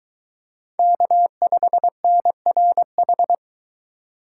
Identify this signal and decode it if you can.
Morse code, heard as a keyed tone, about 23 wpm, 700 Hz. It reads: K5NRH